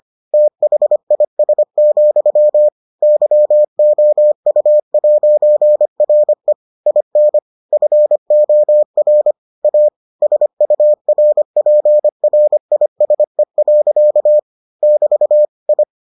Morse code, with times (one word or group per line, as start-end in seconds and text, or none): 0.3-2.7 THIS,
3.0-6.5 YOU'RE
6.9-7.4 IN
7.7-9.3 FOR
9.6-9.9 A
10.2-14.4 SURPRISE.
14.8-15.4 <BT>
15.7-15.8 I